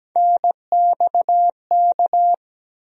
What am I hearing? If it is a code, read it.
Morse code, heard as a keyed tone, about 17 wpm, 700 Hz. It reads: NXK
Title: USA Call Sign Postfixes - 3 Letters 17wpm